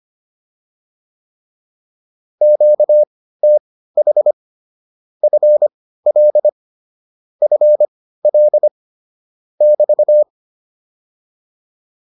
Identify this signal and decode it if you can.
Morse code, heard as a keyed tone, about 25 wpm, spaced as for 15 wpm, 600 Hz. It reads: QTH FL FL <BT>